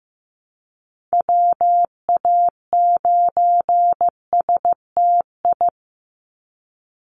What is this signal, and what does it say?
Morse code, heard as a keyed tone, about 15 wpm, 700 Hz. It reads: WA9STI
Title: NA SOTA Activator Callsigns 15wpm